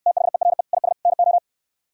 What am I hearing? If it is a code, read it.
Morse code, heard as a keyed tone, about 69 wpm, 700 Hz. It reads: THERE IS NO